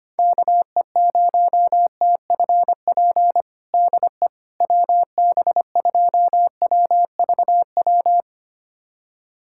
Morse code, heard as a keyed tone, about 25 wpm, 700 Hz. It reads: KE0TFP DE WB2WVW
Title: USA Callsign DE Callsign - 6 Characters - 25wpm